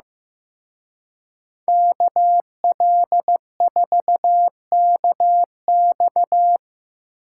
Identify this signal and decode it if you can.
Morse code, heard as a keyed tone, about 18 wpm, 700 Hz. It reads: KL4KX
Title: Speed Racing - USA Call Signs 15wpm